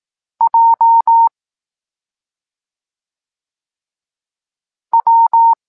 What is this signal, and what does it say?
Morse code, heard as a keyed tone, about 18 wpm, 920 Hz. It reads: J W
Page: Tekst 23.3 – Ciąg dalszy doskonalenia odbioru poznanych liter